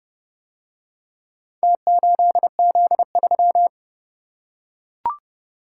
Morse code, heard as a keyed tone, about 30 wpm, 700 Hz. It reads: T8Z3